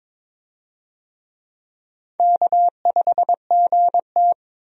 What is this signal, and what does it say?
Morse code, heard as a keyed tone, about 21 wpm, 700 Hz. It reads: K5GT
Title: USA Call Signs - 4 Characters - 22wpm